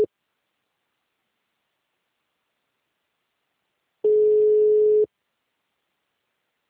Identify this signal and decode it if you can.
Ringback tone (Europe, 425 Hz).
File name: out- -2018- - - .98292